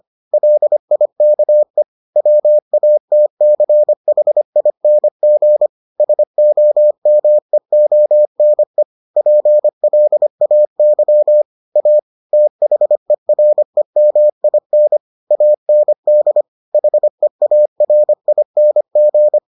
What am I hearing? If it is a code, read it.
Morse code, heard as a keyed tone, about 25 wpm, 600 Hz. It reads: LIKE WATCHING SOMEONE PLAY A THEREMIN AND HEARING